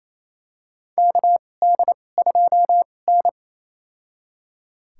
Morse code, heard as a keyed tone, about 27 wpm, spaced as for 20 wpm, 700 Hz. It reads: KD2N
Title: USA Call Signs - 4 Characters - 2x Character Spacing - 28wpm